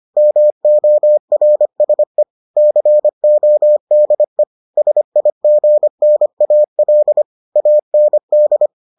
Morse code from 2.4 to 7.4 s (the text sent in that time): CODE SIGNAL